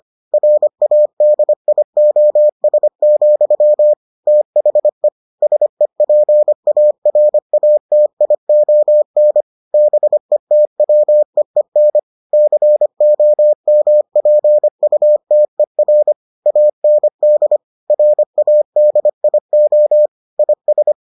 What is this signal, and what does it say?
Morse code, heard as a keyed tone, about 25 wpm, 600 Hz. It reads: RADIOS, THE SEPARATION BETWEEN COMPUTER AND RADIO IS